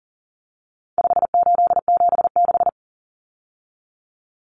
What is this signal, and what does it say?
Morse code, heard as a keyed tone, about 37 wpm, 700 Hz. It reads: 5876
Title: Numbers - 4-Digit 40wpm